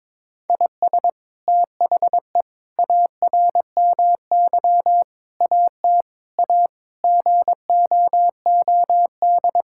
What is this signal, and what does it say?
Morse code, heard as a keyed tone, about 22 wpm, 700 Hz. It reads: IS THE ARMY AT A GOOD
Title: Sentences from Top 1000 Words - Review Part 6 - 22wpm